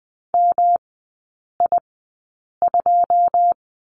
Morse code, heard as a keyed tone, about 20 wpm, spaced as for 8 wpm, 700 Hz. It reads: MI2
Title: 121 Lesson 20e - Number 3 - Three Characters 20wpm